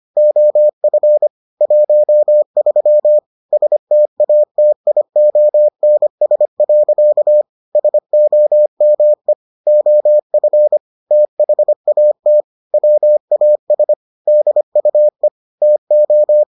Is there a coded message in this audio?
Yes: Morse code from 0.2 to 16.4 s, heard as a keyed tone, about 25 wpm, 600 Hz. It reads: OF 13 STATIONS. SOME OF THAT WAS DUE TO